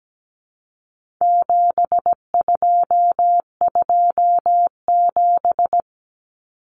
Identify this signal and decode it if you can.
Morse code, heard as a keyed tone, about 17 wpm, 700 Hz. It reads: 7227
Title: Common Cut Numbers - 4-Digit 17wpm